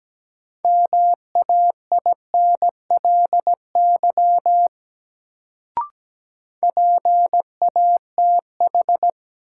Morse code, heard as a keyed tone, about 17 wpm, 700 Hz. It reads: MAINLY PATH
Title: Sentences from Top 1400 Words - New Word Drill - 17wpm